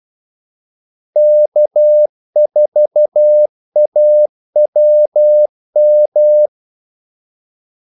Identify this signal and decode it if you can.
Morse code, heard as a keyed tone, about 12 wpm, 600 Hz. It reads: K4AWM
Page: 2 ekstra ordmellemrum mellem kaldesignalerne